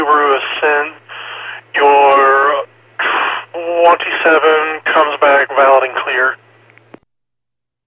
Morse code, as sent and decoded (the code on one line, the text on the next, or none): none